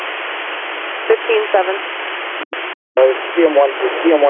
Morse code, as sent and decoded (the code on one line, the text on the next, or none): none